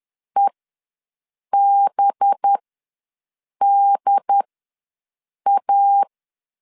Morse code, heard as a keyed tone, about 10 wpm, spaced as for 6 wpm, 800 Hz. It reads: EBDA
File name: LEZIONI / 5 LBVC / 5 LB ws 10wpm 3616